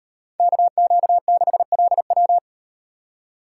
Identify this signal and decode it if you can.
Morse code, heard as a keyed tone, about 36 wpm, 700 Hz. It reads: KQ6LW